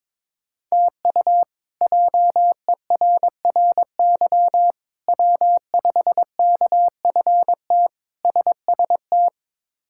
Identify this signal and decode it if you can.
Morse code, heard as a keyed tone, about 22 wpm, 700 Hz. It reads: TU JERRY W5KFT SST
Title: K1USN SST QSO Practice - 22wpm